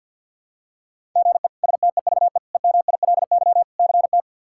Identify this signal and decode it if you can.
Morse code, heard as a keyed tone, about 50 wpm, 700 Hz. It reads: GE STEVE EMILY CT
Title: K1USN SST QSO Practice - 50wpm